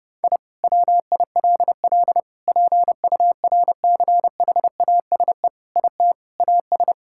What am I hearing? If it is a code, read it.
Morse code, heard as a keyed tone, about 30 wpm, 700 Hz. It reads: I WILL PURCHASE IT AS
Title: Sentences from Top 1000 Words - Review Part 5 - 30wpm